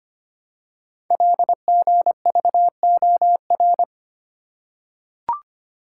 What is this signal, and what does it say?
Morse code, heard as a keyed tone, about 25 wpm, 700 Hz. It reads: LGVOR
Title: Code Groups - 5 Letters - 25wpm